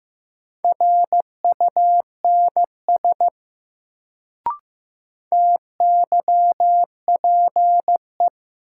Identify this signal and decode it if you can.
Morse code, heard as a keyed tone, about 15 wpm, 700 Hz. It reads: RUNS TYPE
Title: Sentences from Top 600 Words - New Word Drill - 15wpm